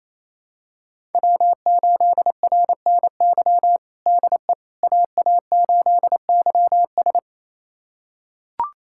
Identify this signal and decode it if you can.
Morse code, heard as a keyed tone, about 28 wpm, 700 Hz. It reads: W8RNY DE AA8YS
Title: USA Callsign DE Callsign 28wpm